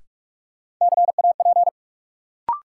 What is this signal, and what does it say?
Morse code, heard as a keyed tone, about 46 wpm, 700 Hz. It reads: CAP
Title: Sentences from Top 1600 Words - New Word Drill - 45wpm